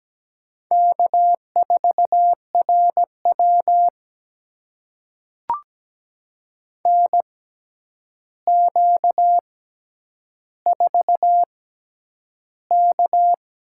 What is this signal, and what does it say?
Morse code, heard as a keyed tone, about 17 wpm, spaced as for 9 wpm, 700 Hz. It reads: K4RW NQ4K
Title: USA Call Signs - 4 Characters - 6x Character Spacing - 17wpm